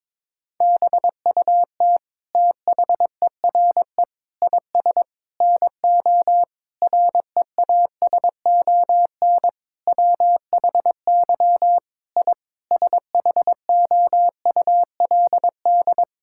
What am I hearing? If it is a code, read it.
Morse code, heard as a keyed tone, about 21 wpm, 700 Hz. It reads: BUT THERE IS NO REASON WHY I SHOULD
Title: Sentences from Top 1000 Words - Review Part 6 - 22wpm